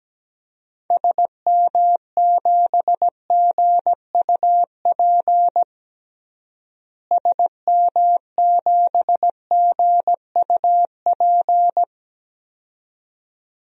Morse code, heard as a keyed tone, about 17 wpm, 700 Hz. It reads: SM7GUP SM7GUP
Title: DXCC-100 Call Signs - 6 Characters - Repeated 2x - 17wpm